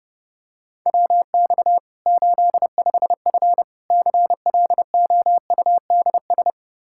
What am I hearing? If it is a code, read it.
Morse code, heard as a keyed tone, about 32 wpm, 700 Hz. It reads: WX 85F CLOUDS
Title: Practice QSO - Exchange 2 - Info - 30wpm